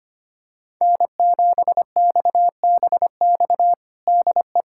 Morse code, heard as a keyed tone, about 25 wpm, 700 Hz. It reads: N7XBX DE
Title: USA Callsign DE Callsign - 5 Characters - 25wpm